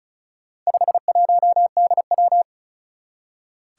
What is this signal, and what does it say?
Morse code, heard as a keyed tone, about 36 wpm, 690 Hz. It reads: 51DW
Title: Four Challenging Letter-Number Combinations 35wpm